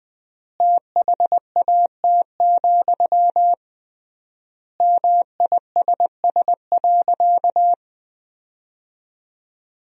Morse code, heard as a keyed tone, about 20 wpm, 700 Hz. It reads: THAT, MISS.
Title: 3x Word-Spacing - Sentences from Top 500 Words - 20wpm